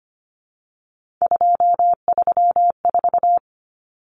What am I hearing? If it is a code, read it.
Morse code, heard as a keyed tone, about 25 wpm, 700 Hz. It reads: 234
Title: Numbers - 3-Digit 25wpm